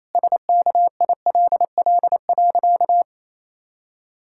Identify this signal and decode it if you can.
Morse code, heard as a keyed tone, about 28 wpm, 700 Hz. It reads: SKILL.
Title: Pangram Challenge v2 - 28wpm